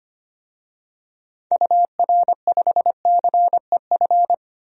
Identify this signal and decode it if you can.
Morse code, heard as a keyed tone, about 25 wpm, 700 Hz. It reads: UR5CEF
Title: DXCC-100 Call Signs - 6 Characters - Repeated 3x - 25wpm